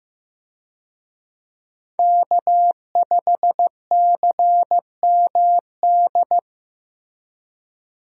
Morse code, heard as a keyed tone, about 15 wpm, 700 Hz. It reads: K5CMD
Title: USA Call Signs - 5 Characters - Repeated 2x - 15wpm